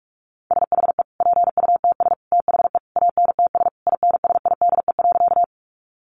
Morse code, heard as a keyed tone, about 44 wpm, 700 Hz. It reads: SHE PUTS THE ANTS INSIDE.
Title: Easy Phrases 45wpm